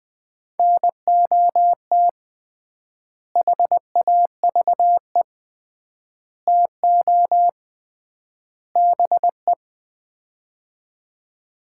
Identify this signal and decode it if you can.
Morse code, heard as a keyed tone, about 20 wpm, 700 Hz. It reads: NOT HAVE TO BE
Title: 3x Word-Spacing - Sets of 4 Words - 20wpm